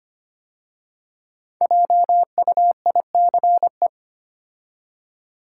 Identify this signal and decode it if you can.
Morse code, heard as a keyed tone, about 25 wpm, 700 Hz. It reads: JUICE
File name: 5 Letter Words - First 3 Letters Given - 25wpm